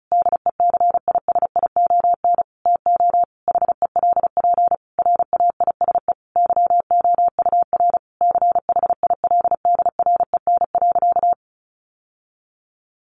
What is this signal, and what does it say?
Morse code, heard as a keyed tone, about 34 wpm, 700 Hz. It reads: DECISION TO HELP RAISE YOUR CHILDREN.
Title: Sentences from Top 1400 Words 35wpm